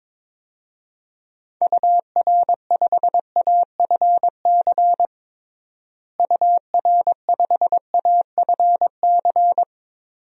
Morse code, heard as a keyed tone, about 22 wpm, 700 Hz. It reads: UR5AFC UR5AFC
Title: DXCC-100 Call Signs - 6 Characters - Repeated 2x - 22wpm